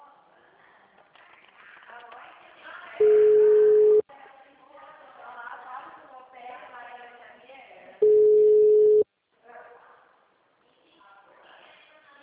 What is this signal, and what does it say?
Ringback tone (Europe, 425 Hz), with speech.